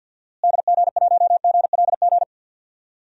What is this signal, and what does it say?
Morse code, heard as a keyed tone, about 47 wpm, 700 Hz. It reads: DG1GLG
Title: DXCC-100 Call Signs - 6 Characters - Repeated 2x - 50wpm